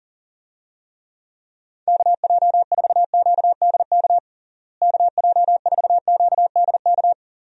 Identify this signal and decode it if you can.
Morse code, heard as a keyed tone, about 40 wpm, 700 Hz. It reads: KJ4QDK KJ4QDK
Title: USA Call Signs - 6 Characters - Repeated 3x - 40wpm